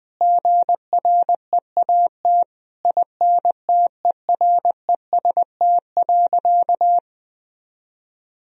Morse code, heard as a keyed tone, about 20 wpm, 700 Hz. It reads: GREAT INTEREST.
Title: Sentences from Top 1000 Words - Review Part 2 - 20wpm